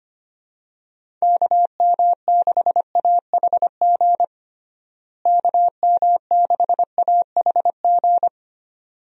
Morse code, heard as a keyed tone, about 25 wpm, 700 Hz. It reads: KM6AHG KM6AHG
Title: USA Call Signs - 6 Characters - Repeated 2x - 25wpm